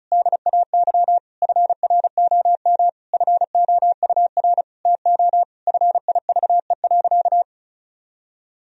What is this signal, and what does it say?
Morse code, heard as a keyed tone, about 34 wpm, 700 Hz. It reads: DAY FROM FOUR TO FIVE.
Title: Sentences from Top 1000 Words - Review Part 2 - 35wpm